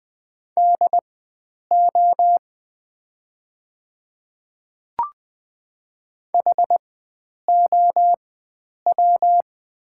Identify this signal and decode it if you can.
Morse code, heard as a keyed tone, about 20 wpm, spaced as for 9 wpm, 700 Hz. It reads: DO HOW
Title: 4x Character Spacing - Top 100 Words - 20wpm